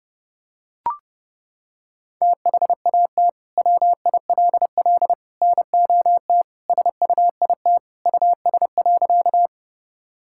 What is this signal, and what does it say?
Morse code, heard as a keyed tone, about 32 wpm, 700 Hz. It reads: THAT WILL NOT SUIT US.
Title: Sentences from Top 1600 Words 30wpm